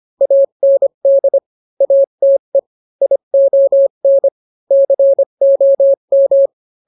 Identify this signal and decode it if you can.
Morse code, heard as a keyed tone, about 25 wpm, 550 Hz. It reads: AND ATE ION COM